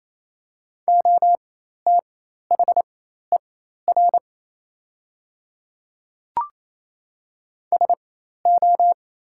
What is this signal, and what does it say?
Morse code, heard as a keyed tone, about 27 wpm, spaced as for 13 wpm, 690 Hz. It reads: OTHER SO
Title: 4x Character Spacing - Top 100 Words - 28wpm